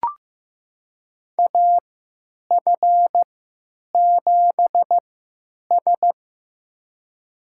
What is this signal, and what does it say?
Morse code, heard as a keyed tone, about 16 wpm, spaced as for 9 wpm, 700 Hz. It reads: AF7S
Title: USA Call Signs - 3x Character Spacing - 15wpm